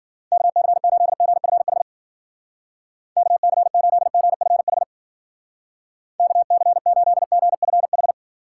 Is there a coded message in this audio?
Yes: Morse code from 0.3 to 8.1 s, heard as a keyed tone, about 54 wpm, 700 Hz. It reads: KC8GFH KC8GFH KC8GFH